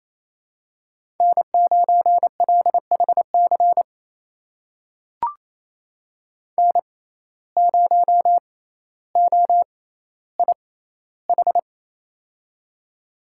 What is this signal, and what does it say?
Morse code, heard as a keyed tone, about 28 wpm, spaced as for 14 wpm, 700 Hz. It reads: N9LHC N0OIH
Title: USA Call Signs - 5 Characters - 6x Character Spacing - 28wpm